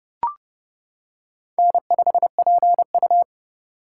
Morse code, heard as a keyed tone, about 30 wpm, 700 Hz. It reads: N5PU
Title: K1USN SST QSO Practice - 30wpm